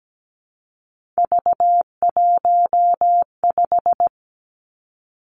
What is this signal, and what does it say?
Morse code, heard as a keyed tone, about 17 wpm, 700 Hz. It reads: V15
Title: Three Challenging Letter-Number Combinations 17wpm